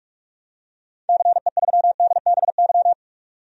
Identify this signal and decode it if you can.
Morse code, heard as a keyed tone, about 42 wpm, 700 Hz. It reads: CE3DBY
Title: DXCC-100 Call Signs - 6 Characters - 45wpm